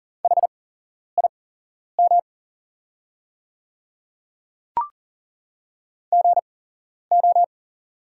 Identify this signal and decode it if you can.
Morse code, heard as a keyed tone, about 38 wpm, spaced as for 11 wpm, 690 Hz. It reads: HIM GO
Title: 8x Character Spacing - Top 100 Words - 40wpm